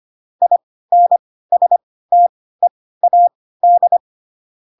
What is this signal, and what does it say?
Morse code, heard as a keyed tone, about 25 wpm, spaced as for 16 wpm, 700 Hz. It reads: INSTEAD
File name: Pinocchio / Pinocchio ch5 10 25 ik1hge